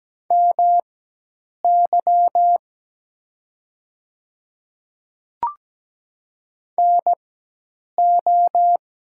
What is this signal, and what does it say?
Morse code, heard as a keyed tone, about 17 wpm, spaced as for 8 wpm, 700 Hz. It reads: MY NO
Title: 4x Character Spacing - Top 100 Words - 17wpm